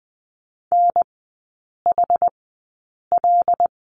Morse code, heard as a keyed tone, about 20 wpm, spaced as for 8 wpm, 700 Hz. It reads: NHL
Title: 121 Lesson 20e - Number 3 - Three Characters 20wpm